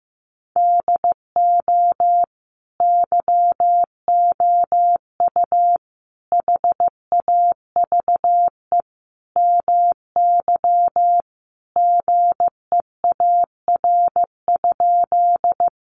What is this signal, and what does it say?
Morse code, heard as a keyed tone, about 15 wpm, 700 Hz. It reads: DO YOU HAVE MY GEAR?
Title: Sentences from Top 1400 Words 15wpm